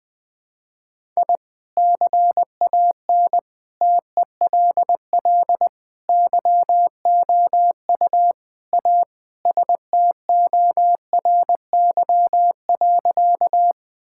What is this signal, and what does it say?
Morse code, heard as a keyed tone, about 20 wpm, 700 Hz. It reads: I CAN TELL YOU A STORY.
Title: Sentences with 4-Word Phrases - Spoken Beginning - 20wpm